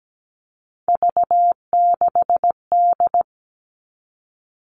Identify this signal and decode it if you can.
Morse code, heard as a keyed tone, about 17 wpm, 700 Hz. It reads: V6D
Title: All Cut Numbers - 3-Digit 17wpm